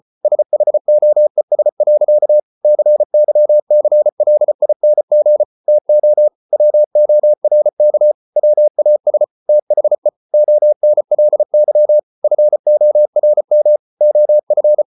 Morse code, heard as a keyed tone, about 34 wpm, 600 Hz. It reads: SHOES. CYCLING TO WORK WAS THE ONLY FORM OF